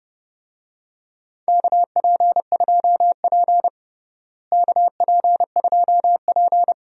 Morse code, heard as a keyed tone, about 30 wpm, 700 Hz. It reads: KP2P KP2P
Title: DXCC-100 Call Signs - 4 Characters - Repeated 2x - 30wpm